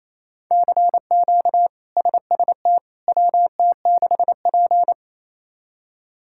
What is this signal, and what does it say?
Morse code, heard as a keyed tone, about 27 wpm, 700 Hz. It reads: CQ SST WT6P
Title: K1USN SST QSO Practice - 28wpm